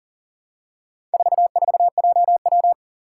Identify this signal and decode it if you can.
Morse code, heard as a keyed tone, about 38 wpm, 700 Hz. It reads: 44JW
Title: Four Challenging Letter-Number Combinations 40wpm